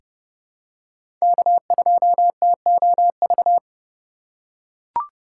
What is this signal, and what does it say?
Morse code, heard as a keyed tone, about 30 wpm, 700 Hz. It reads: K2TOV
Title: USA Call Signs - 5 Characters - 2x Character Spacing - 30wpm